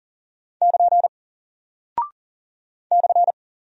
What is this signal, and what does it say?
Morse code, heard as a keyed tone, about 40 wpm, 700 Hz. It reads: <KN> /
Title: Single Character (Letter:Number:Punctuation:Prosign) Alt 40wpm